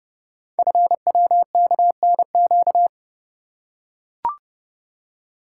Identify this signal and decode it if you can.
Morse code, heard as a keyed tone, about 30 wpm, 700 Hz. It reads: FWKNQ
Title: Code Groups - 5 Letters - 30wpm